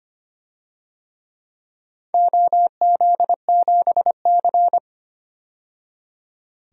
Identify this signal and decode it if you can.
Morse code, heard as a keyed tone, about 25 wpm, 700 Hz. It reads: OZ7C
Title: DXCC-100 Call Signs - 4 Characters - 25wpm